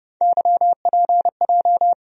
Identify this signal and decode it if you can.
Morse code, heard as a keyed tone, about 30 wpm, 700 Hz. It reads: YPJ
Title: USA Call Sign Postfixes - 3 Letters 30wpm